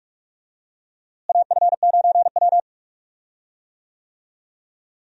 Morse code, heard as a keyed tone, about 43 wpm, 700 Hz. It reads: AF9W
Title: USA Call Signs - 4 Characters - Repeated 2x - 45wpm